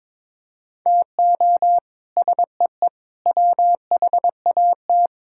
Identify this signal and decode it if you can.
Morse code, heard as a keyed tone, about 21 wpm, 700 Hz. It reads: TO SEE WHAT
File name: practice / 22wpm / Sets of 8 Words - 22wpm